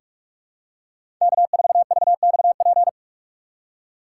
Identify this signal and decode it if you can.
Morse code, heard as a keyed tone, about 42 wpm, 700 Hz. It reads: K4VXP
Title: USA Call Signs - 2x Character Spacing - 45wpm